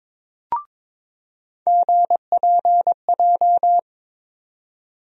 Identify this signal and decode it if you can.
Morse code, heard as a keyed tone, about 22 wpm, 700 Hz. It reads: GPJ
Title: USA Call Sign Postfixes - 3 Letters 22wpm